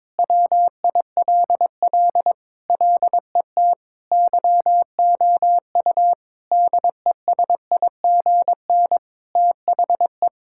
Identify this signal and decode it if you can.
Morse code, heard as a keyed tone, about 22 wpm, 700 Hz. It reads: WILL LET YOU DESIGN THE
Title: Sentences from Top 1000 Words - Review Part 5 - 22wpm